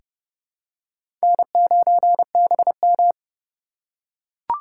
Morse code, heard as a keyed tone, about 30 wpm, 700 Hz. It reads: N9BM
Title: USA Call Signs - 4 Characters - 3x Character Spacing - 30wpm